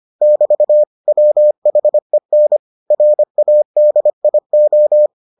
Morse code, heard as a keyed tone, about 25 wpm, 600 Hz. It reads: <BT> WHEN RADIO